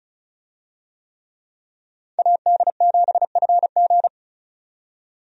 Morse code, heard as a keyed tone, about 34 wpm, 700 Hz. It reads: AD7FG
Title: USA Call Signs - 5 Characters - 35wpm